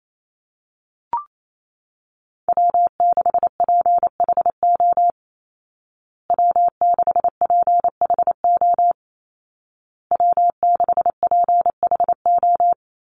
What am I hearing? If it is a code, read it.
Morse code, heard as a keyed tone, about 28 wpm, 700 Hz. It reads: W6PHO W6PHO W6PHO
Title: Call Signs Repeated 2x - 28wpm